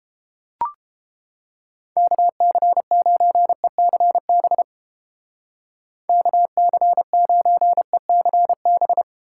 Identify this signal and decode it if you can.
Morse code, heard as a keyed tone, about 31 wpm, 700 Hz. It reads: KC9ECB KC9ECB